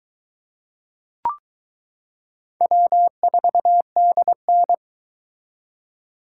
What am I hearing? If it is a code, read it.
Morse code, heard as a keyed tone, about 21 wpm, 700 Hz. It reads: W4DN